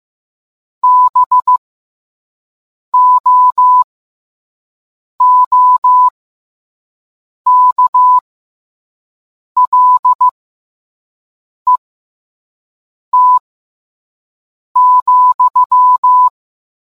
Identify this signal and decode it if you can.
Morse code, heard as a keyed tone, about 15 wpm, spaced as for 5 wpm, 1000 Hz. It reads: BOOKLET,